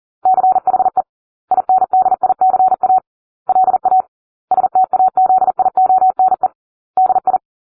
Morse code, heard as a keyed tone, about 40 wpm, 750 Hz. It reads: CHE INDICA LA STAZIONE DI